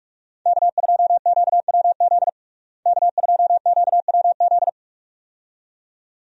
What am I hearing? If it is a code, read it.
Morse code, heard as a keyed tone, about 41 wpm, 700 Hz. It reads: K2QWZ K2QWZ
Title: USA Call Signs - 5 Characters - Repeated 2x - 45wpm